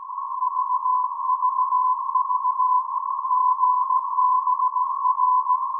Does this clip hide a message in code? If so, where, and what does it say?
Yes: Morse code from 0.4 to 5.6 s, heard as a keyed tone, about 17 wpm, 1010 Hz. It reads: VVV GBR